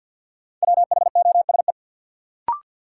Morse code, heard as a keyed tone, about 48 wpm, 700 Hz. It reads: WHOSE
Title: Sentences from Top 1200 Words - New Word Drill Alt - 50wpm